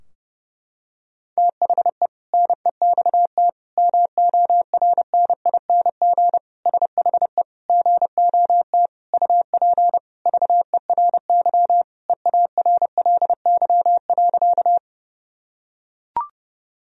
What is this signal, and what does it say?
Morse code, heard as a keyed tone, about 30 wpm, 700 Hz. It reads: THE NEXT MORNING SHE GOT UP VERY EARLY.
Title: Sentences from Top 1000 Words - Review Part 2 - 30wpm